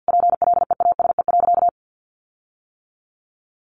Morse code, heard as a keyed tone, about 50 wpm, 700 Hz. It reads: PLEASE.
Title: Sentences from Top 1500 Words 50wpm